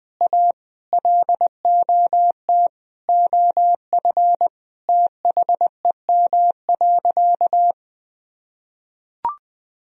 Morse code, heard as a keyed tone, about 20 wpm, 700 Hz. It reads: A LOT OF THEM.
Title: Sentences from Top 1000 Words - Review Part 3 - 20wpm